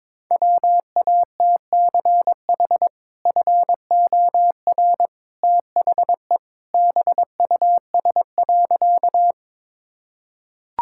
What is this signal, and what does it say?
Morse code, heard as a keyed tone, about 22 wpm, 700 Hz. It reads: WATCH FOR THE BUS.
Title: Sentences from Top 1000 Words - Review Part 6 - 22wpm